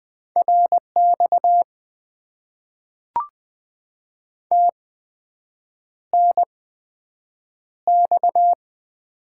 Morse code, heard as a keyed tone, about 20 wpm, spaced as for 7 wpm, 700 Hz. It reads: RX TNX
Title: Top 90 QSO Elements - 8x Character Spacing - 20wpm